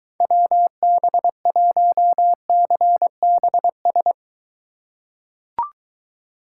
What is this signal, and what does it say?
Morse code, heard as a keyed tone, about 21 wpm, 700 Hz. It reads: WB1CBS